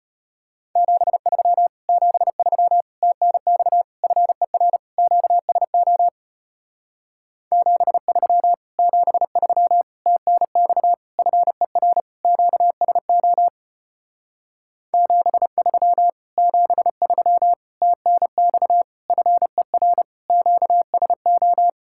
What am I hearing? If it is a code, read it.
Morse code, heard as a keyed tone, about 33 wpm, 700 Hz. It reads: 73 73 TNX FER QSO 73 73 TNX FER QSO 73 73 TNX FER QSO